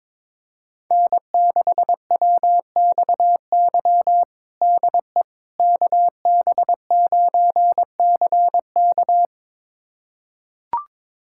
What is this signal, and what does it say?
Morse code, heard as a keyed tone, about 22 wpm, 700 Hz. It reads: N6WXY DE KB9CK
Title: USA Callsign DE Callsign - 5 Characters - 22wpm